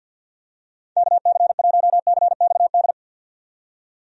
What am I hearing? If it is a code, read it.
Morse code, heard as a keyed tone, about 51 wpm, 700 Hz. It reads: KC1CXD
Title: USA Call Signs - 6 Characters - 50wpm